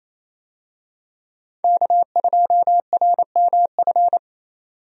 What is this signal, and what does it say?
Morse code, heard as a keyed tone, about 28 wpm, 700 Hz. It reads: K2RMF
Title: USA Call Signs - 5 Characters - Repeated 2x - 28wpm